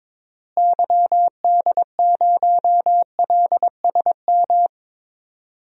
Morse code, heard as a keyed tone, about 22 wpm, 700 Hz. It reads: YD0LSM
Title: DXCC-100 Call Signs - 6 Characters - Repeated 3x - 22wpm